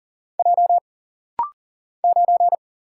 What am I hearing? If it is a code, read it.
Morse code, heard as a keyed tone, about 40 wpm, 700 Hz. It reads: J 9
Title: Single Letter-Number Alt - Rapid-Fire 40wpm